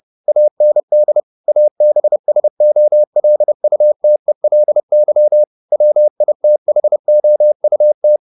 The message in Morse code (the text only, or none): AND ABSOLUTELY WITHOUT